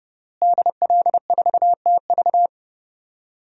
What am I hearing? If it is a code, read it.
Morse code, heard as a keyed tone, about 28 wpm, 700 Hz. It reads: DL4TV
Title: DXCC-100 Call Signs - 5 Characters - Repeated 3x - 30wpm